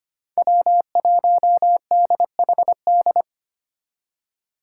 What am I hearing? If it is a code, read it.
Morse code, heard as a keyed tone, about 30 wpm, 700 Hz. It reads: W1DHD
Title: Speed Racing - USA 1x3 Call Signs 25wpm